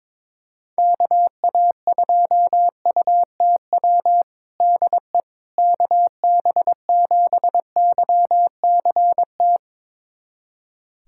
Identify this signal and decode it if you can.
Morse code, heard as a keyed tone, about 21 wpm, 700 Hz. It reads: KA2UTW DE KB7YCT
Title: USA Callsign DE Callsign 22wpm